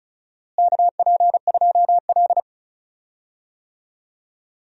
Morse code, heard as a keyed tone, about 34 wpm, 700 Hz. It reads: KP2L
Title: DXCC-100 Call Signs - 4 Characters - Repeated 2x - 35wpm